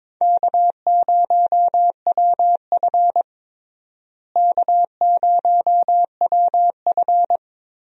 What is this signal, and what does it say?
Morse code, heard as a keyed tone, about 22 wpm, 700 Hz. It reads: K0WF K0WF
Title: USA Call Signs - 4 Characters - Repeated 3x - 22wpm